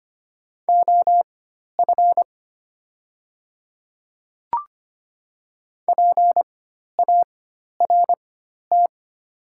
Morse code, heard as a keyed tone, about 25 wpm, spaced as for 12 wpm, 690 Hz. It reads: OF PART
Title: 4x Character Spacing - Top 300 Words - 25wpm